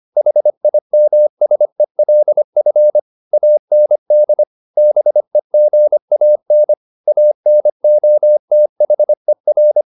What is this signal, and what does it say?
Morse code, heard as a keyed tone, about 25 wpm, 600 Hz. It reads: HIMSELF AND BEGAN ANOTHER